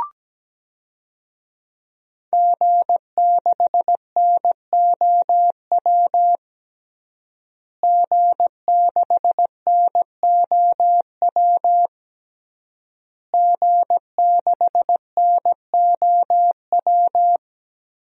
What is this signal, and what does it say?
Morse code, heard as a keyed tone, about 17 wpm, 700 Hz. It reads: G6NOW G6NOW G6NOW
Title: DXCC-100 Call Signs - 5 Characters - Repeated 3x - 17wpm